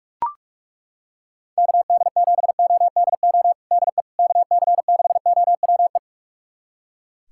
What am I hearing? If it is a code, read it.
Morse code, heard as a keyed tone, about 42 wpm, 700 Hz. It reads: KD7ODO DE KC6OWE
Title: USA Callsign DE Callsign - 6 Characters - 45wpm